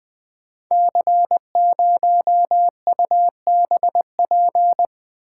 Morse code, heard as a keyed tone, about 20 wpm, 700 Hz. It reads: C0UBP
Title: Code Groups - 5 Letter:Numbers - 20wpm